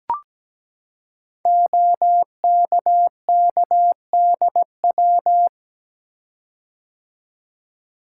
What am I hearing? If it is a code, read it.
Morse code, heard as a keyed tone, about 17 wpm, 700 Hz. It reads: OKKDW
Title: Code Groups - 5 Letter:Numbers - 17wpm